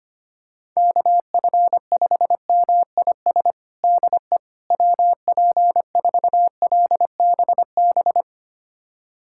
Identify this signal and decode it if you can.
Morse code, heard as a keyed tone, about 25 wpm, 700 Hz. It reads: KF5MIS DE WP4LBB
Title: USA Callsign DE Callsign - 6 Characters - 25wpm